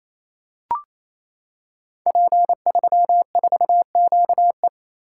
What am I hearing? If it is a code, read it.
Morse code, heard as a keyed tone, about 27 wpm, 700 Hz. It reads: P34QE
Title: Code Groups - 5 Letter:Numbers - 28wpm